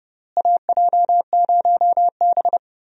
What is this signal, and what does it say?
Morse code, heard as a keyed tone, about 32 wpm, 700 Hz. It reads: AJ0B
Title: USA Call Signs - 4 Characters - Repeated 3x - 30wpm